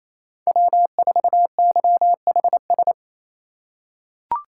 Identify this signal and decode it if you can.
Morse code, heard as a keyed tone, about 28 wpm, 700 Hz. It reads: W4YHS
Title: USA Call Signs - 4x Character Spacing - 28wpm